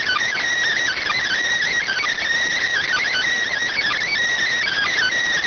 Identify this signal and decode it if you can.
Emergency Alert System (SAME) tones (header bursts), heard as harsh buzzing bursts. It is heard from 0.8 to 5.2 s.